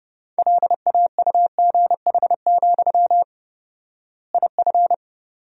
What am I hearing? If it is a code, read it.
Morse code, heard as a keyed tone, about 32 wpm, 700 Hz. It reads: LAUGH, IF
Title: 4x Word-Spacing - Sentences from Top 500 Words - 30wpm